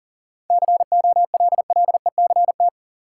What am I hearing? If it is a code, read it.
Morse code, heard as a keyed tone, about 40 wpm, 700 Hz. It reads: COLLECT
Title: Sentences from Top 1600 Words - New Word Drill Alt - 40wpm